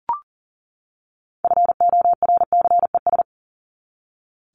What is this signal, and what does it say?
Morse code, heard as a keyed tone, about 40 wpm, 700 Hz. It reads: FORCES
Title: ES:S Suffix - Top 100 Words with Base Words 40wpm